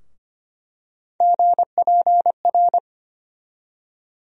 Morse code, heard as a keyed tone, about 25 wpm, 700 Hz. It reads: GPR
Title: USA Call Sign Postfixes - 3 Letters 25wpm